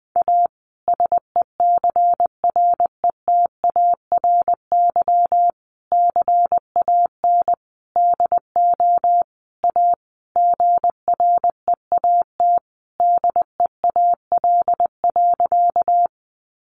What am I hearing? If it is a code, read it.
Morse code, heard as a keyed tone, about 20 wpm, 700 Hz. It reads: A SECRETARY CAN DO A GREAT DEAL.
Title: Sentences from Top 1300 Words 20wpm